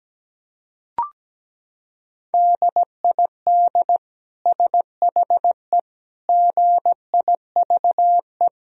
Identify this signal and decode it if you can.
Morse code, heard as a keyed tone, about 17 wpm, 700 Hz. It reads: DID SHE GIVE
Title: Sentences from Top 1000 Words - Review Part 4 - 17wpm